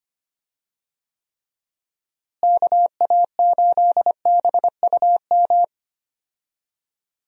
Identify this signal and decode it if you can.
Morse code, heard as a keyed tone, about 25 wpm, 700 Hz. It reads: KA8BUM
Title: USA Call Signs - 6 Characters - Repeated 3x - 25wpm